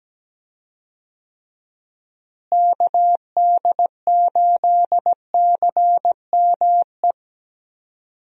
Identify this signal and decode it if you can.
Morse code, heard as a keyed tone, about 17 wpm, 700 Hz. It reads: KD8CME
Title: USA Call Signs - 6 Characters - Repeated 3x - 17wpm